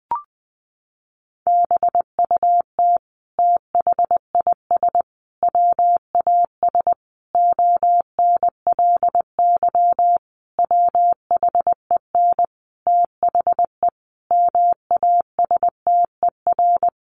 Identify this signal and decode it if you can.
Morse code, heard as a keyed tone, about 20 wpm, 700 Hz. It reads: BUT THIS WAS ONLY WHEN THE MASTER
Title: Sentences from Top 1300 Words 20wpm